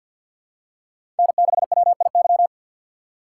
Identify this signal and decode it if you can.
Morse code, heard as a keyed tone, about 51 wpm, 700 Hz. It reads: N6WIY
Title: USA Call Signs - 5 Characters - 2x Character Spacing - 50wpm